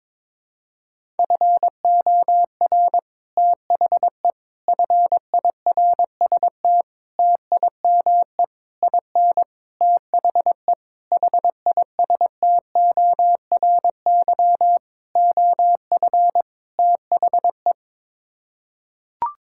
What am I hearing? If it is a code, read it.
Morse code, heard as a keyed tone, about 22 wpm, 700 Hz. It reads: FOR THE FIRST TIME IN THE HISTORY OF THE
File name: Sets of 9 Words - 22wpm